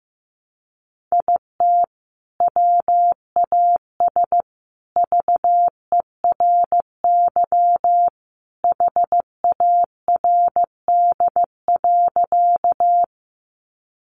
Morse code, heard as a keyed tone, about 15 wpm, 700 Hz. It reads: IT WAS VERY HARD.
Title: Sentences from Top 300 Words 15wpm